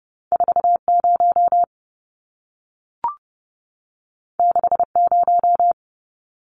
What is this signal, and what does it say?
Morse code, heard as a keyed tone, about 30 wpm, 700 Hz. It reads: 40 60
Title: Numbers - 2-Digit 30wpm